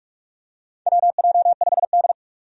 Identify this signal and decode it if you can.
Morse code, heard as a keyed tone, about 46 wpm, 700 Hz. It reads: WJ5D
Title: USA Call Signs - 4 Characters - 45wpm